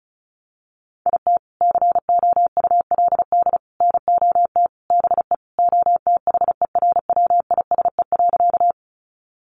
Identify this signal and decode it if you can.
Morse code, heard as a keyed tone, about 34 wpm, 700 Hz. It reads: IT COULD NOT BE OTHERWISE.
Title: Sentences from Top 1200 Words 35wpm